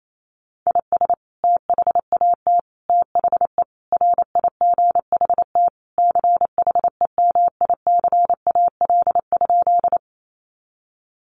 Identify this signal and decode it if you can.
Morse code, heard as a keyed tone, about 28 wpm, 700 Hz. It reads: IS THAT THE RIGHT CHEMICAL?
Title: Sentences from Top 1200 Words 28wpm